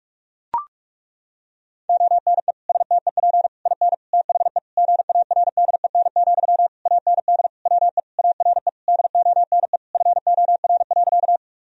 Morse code, heard as a keyed tone, about 41 wpm, 700 Hz. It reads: ONE STEP IN THE GARDEN, AND WE ARE DONE FOR.
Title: Sentences from Top 1000 Words - Review Part 4 - 45wpm